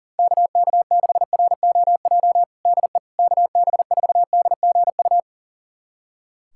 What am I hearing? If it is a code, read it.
Morse code, heard as a keyed tone, about 40 wpm, 700 Hz. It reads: KK6ROJ DE KB4DGU
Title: USA Callsign DE Callsign - 6 Characters - 40wpm